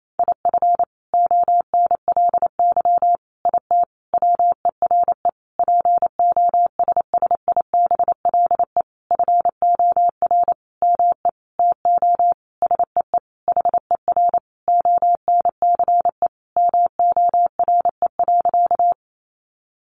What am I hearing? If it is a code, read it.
Morse code, heard as a keyed tone, about 28 wpm, 700 Hz. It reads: IF ONLY IT WERE POSSIBLE FOR ME TO SEE HER ONCE MORE.
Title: Sentences from Top 500 Words 28wpm